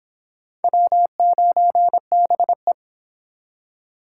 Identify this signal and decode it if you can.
Morse code, heard as a keyed tone, about 23 wpm, 700 Hz. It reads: W9BE